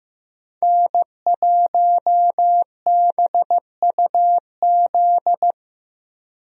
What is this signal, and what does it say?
Morse code, heard as a keyed tone, about 15 wpm, 700 Hz. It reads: N1BUZ
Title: USA Call Signs - 5 Characters - Repeated 2x - 15wpm